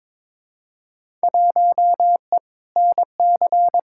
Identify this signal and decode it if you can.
Morse code, heard as a keyed tone, about 22 wpm, 700 Hz. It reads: 1E NC
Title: Field Day Exchanges - 22wpm